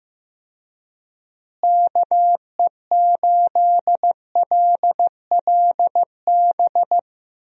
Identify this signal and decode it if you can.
Morse code, heard as a keyed tone, about 15 wpm, 700 Hz. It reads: KE8LLB
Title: USA Call Signs - 6 Characters - Repeated 2x - 15wpm